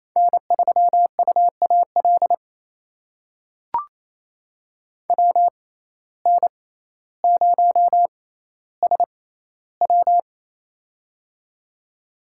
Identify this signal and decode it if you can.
Morse code, heard as a keyed tone, about 28 wpm, spaced as for 14 wpm, 700 Hz. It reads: N3UAL WN0SW
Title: USA Call Signs - 5 Characters - 6x Character Spacing - 28wpm